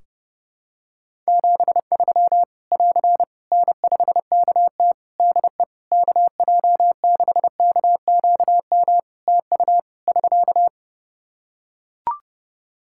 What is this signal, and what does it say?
Morse code, heard as a keyed tone, about 30 wpm, 700 Hz. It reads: 73 <AR> N5KT DE KJ6KQM TU <SK>
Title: Practice QSO - Exchange 4 - Ending - 30wpm